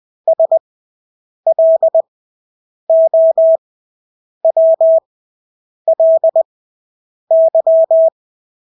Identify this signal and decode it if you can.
Morse code, heard as a keyed tone, about 20 wpm, spaced as for 8 wpm, 650 Hz. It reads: SLOWLY
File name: Engelske ord 20/8 wpm